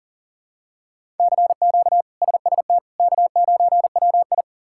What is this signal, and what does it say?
Morse code, heard as a keyed tone, about 37 wpm, 700 Hz. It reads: CQ SST K9WI
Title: K1USN SST QSO Practice - 40wpm